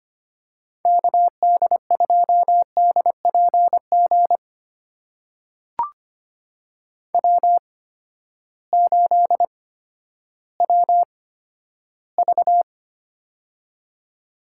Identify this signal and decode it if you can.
Morse code, heard as a keyed tone, about 25 wpm, spaced as for 12 wpm, 700 Hz. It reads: KD2DPG W8WV
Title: USA Call Signs - 8x Character Spacing - 25wpm